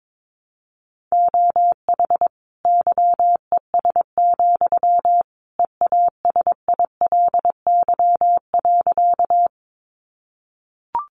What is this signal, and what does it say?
Morse code, heard as a keyed tone, about 22 wpm, 700 Hz. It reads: OH YES, EASILY.
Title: Sentences from Top 900 Words 22wpm